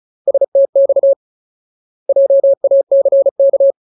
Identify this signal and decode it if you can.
Morse code, heard as a keyed tone, about 34 wpm, 550 Hz. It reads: STX JACK
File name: Ultimate Word List / 35WPM Ultimate